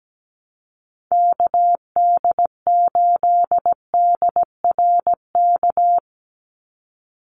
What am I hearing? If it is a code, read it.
Morse code, heard as a keyed tone, about 17 wpm, 700 Hz. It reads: KD8DRK
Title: USA Call Signs 17wpm